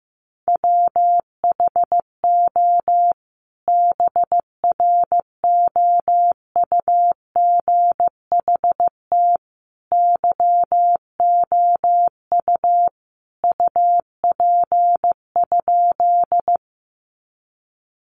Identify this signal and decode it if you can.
Morse code, heard as a keyed tone, about 15 wpm, 700 Hz. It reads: WHO BROUGHT YOU UP?
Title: Sentences from Top 500 Words 15wpm